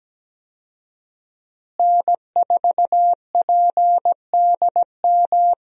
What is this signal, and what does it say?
Morse code, heard as a keyed tone, about 17 wpm, 700 Hz. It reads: N4PDM
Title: USA Call Signs - 5 Characters - Repeated 2x - 17wpm